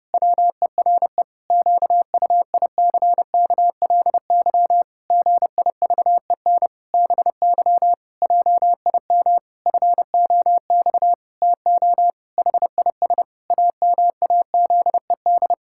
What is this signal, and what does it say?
Morse code, heard as a keyed tone, about 30 wpm, 700 Hz. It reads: WERE QUICKLY GIVEN BY JIM FOX TO HIS AMAZED
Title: Pangram Challenge v2 - 30wpm